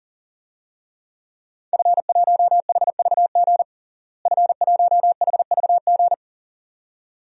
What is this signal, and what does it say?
Morse code, heard as a keyed tone, about 37 wpm, 700 Hz. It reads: F1HVG F1HVG
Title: DXCC-100 Call Signs - 5 Characters - Repeated 2x - 40wpm